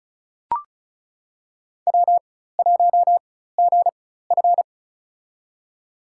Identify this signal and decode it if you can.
Morse code, heard as a keyed tone, about 34 wpm, spaced as for 16 wpm, 700 Hz. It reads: W1GF
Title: USA Call Signs - 4x Character Spacing - 35wpm